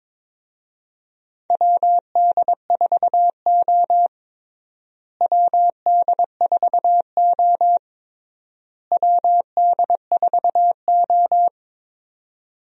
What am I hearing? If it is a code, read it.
Morse code, heard as a keyed tone, about 22 wpm, 700 Hz. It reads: WD4O WD4O WD4O
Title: USA Call Signs - 4 Characters - Repeated 3x - 22wpm